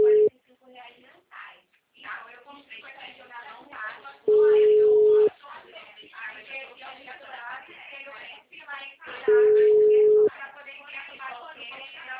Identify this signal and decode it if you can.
Ringback tone (Europe, 425 Hz), with speech.
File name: out- -2011- - - .19358